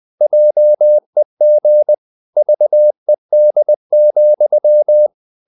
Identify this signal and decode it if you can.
Morse code, heard as a keyed tone, about 20 wpm, 600 Hz. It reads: JEG VED,
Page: Ferie 20 wpm